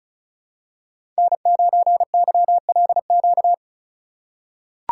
Morse code, heard as a keyed tone, about 36 wpm, 700 Hz. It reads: N9YLQ
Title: USA Call Signs - 5 Characters - 2x Character Spacing - 35wpm